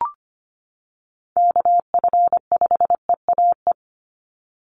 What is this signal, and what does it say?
Morse code, heard as a keyed tone, about 25 wpm, 700 Hz. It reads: KF5EAE
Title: USA Call Signs 25wpm